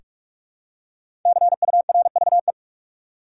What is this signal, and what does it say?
Morse code, heard as a keyed tone, about 47 wpm, 700 Hz. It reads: CURVE
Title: Five Letter Words - 45wpm